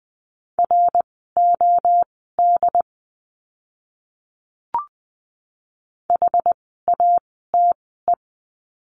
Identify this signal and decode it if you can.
Morse code, heard as a keyed tone, about 20 wpm, spaced as for 14 wpm, 700 Hz. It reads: ROD HATE
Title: 068 Lesson 12f - Letter d - Words 20wpm